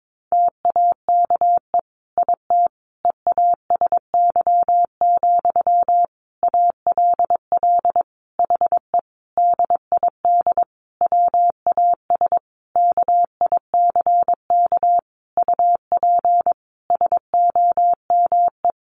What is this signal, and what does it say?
Morse code, heard as a keyed tone, about 22 wpm, 700 Hz. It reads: TAKE IT EASY, ALL HE DID WAS KICK UP SOME
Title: A Sound Of Thunder 22wpm